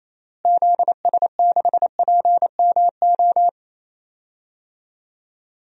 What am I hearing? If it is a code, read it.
Morse code, heard as a keyed tone, about 28 wpm, 700 Hz. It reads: ZS6PMO
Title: DXCC-100 Call Signs - 6 Characters - Repeated 2x - 28wpm